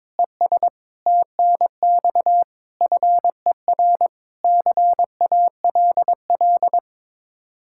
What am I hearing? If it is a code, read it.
Morse code, heard as a keyed tone, about 22 wpm, 700 Hz. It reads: ES TNX FER CALL
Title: Practice QSO - Exchange 1 - Beginning - 22wpm